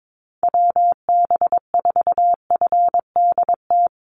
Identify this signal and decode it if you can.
Morse code, heard as a keyed tone, about 22 wpm, 700 Hz. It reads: WB4FDT
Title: Call Signs Repeated 2x - 22wpm